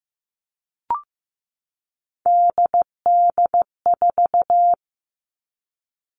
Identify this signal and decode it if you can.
Morse code, heard as a keyed tone, about 15 wpm, 700 Hz. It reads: DD4
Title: Three Challenging Letter-Number Combinations 15wpm